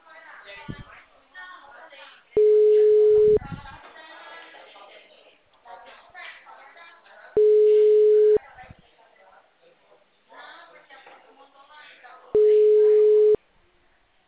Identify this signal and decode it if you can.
Ringback tone (Europe, 425 Hz), with speech.
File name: out- -2015- - - .34266